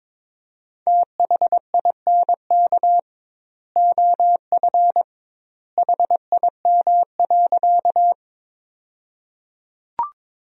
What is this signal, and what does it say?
Morse code, heard as a keyed tone, about 22 wpm, 700 Hz. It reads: THINK OF HIM.
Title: 2x Word-Spacing - Sentences from Top 200 Words - 22wpm